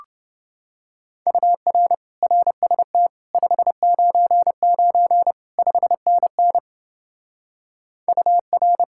Morse code, heard as a keyed tone, about 28 wpm, 700 Hz. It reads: UR RST 599 5NN UR